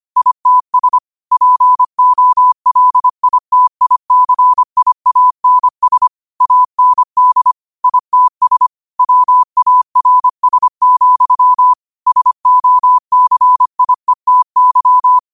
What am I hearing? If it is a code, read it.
Morse code, heard as a keyed tone, about 25 wpm, 1000 Hz. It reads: ITS POLITICIANS AND ITS WARS, SOCIETY
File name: qotdfiles / 25wpm